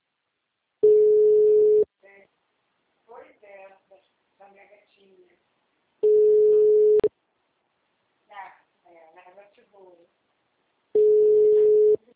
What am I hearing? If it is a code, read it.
Ringback tone (Europe, 425 Hz), with speech.
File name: out- -2011- - - .22799